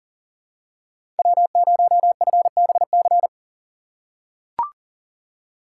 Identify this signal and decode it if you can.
Morse code, heard as a keyed tone, about 39 wpm, 700 Hz. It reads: W0FBC
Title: USA Call Signs - 4x Character Spacing - 40wpm